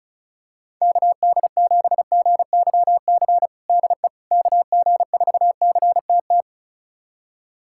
Morse code, heard as a keyed tone, about 36 wpm, 700 Hz. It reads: KD7GYC DE KG4CTT
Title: USA Callsign DE Callsign - 6 Characters - 35wpm